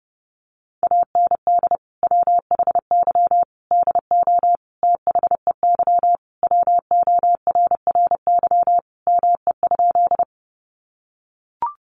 Morse code, heard as a keyed tone, about 30 wpm, 700 Hz. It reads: AND WHY DO THEY WORRY ME?
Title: Sentences from Top 1100 Words 30wpm